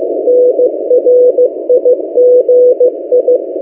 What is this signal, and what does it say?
Morse code, heard as a keyed tone, about 15 wpm, 500 Hz. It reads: NRIGI
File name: M08a- -1355utc-8096khz